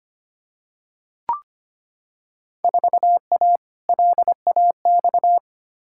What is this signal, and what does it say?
Morse code, heard as a keyed tone, about 25 wpm, 700 Hz. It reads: 4A LAX
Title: Field Day Exchanges - 25wpm